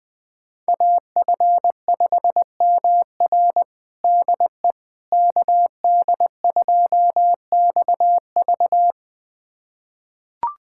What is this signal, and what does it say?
Morse code, heard as a keyed tone, about 21 wpm, 700 Hz. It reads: AF5MR DE KD2XV
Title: USA Callsign DE Callsign - 5 Characters - 20wpm